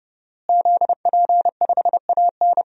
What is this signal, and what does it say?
Morse code, heard as a keyed tone, about 30 wpm, 700 Hz. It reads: ZP5AN
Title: DXCC-100 Call Signs - 5 Characters - Repeated 3x - 30wpm